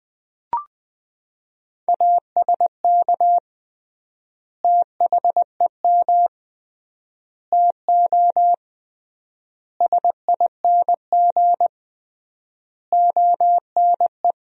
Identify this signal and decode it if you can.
Morse code, heard as a keyed tone, about 20 wpm, 700 Hz. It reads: ASK THEM TO SING ONE
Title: 3x Word-Spacing - Sentences from Top 400 Words - 20wpm